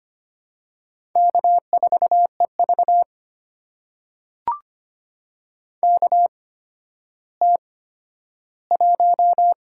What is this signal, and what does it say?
Morse code, heard as a keyed tone, about 25 wpm, spaced as for 12 wpm, 700 Hz. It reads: K4EV KT1
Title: USA Call Signs - 4 Characters - 8x Character Spacing - 25wpm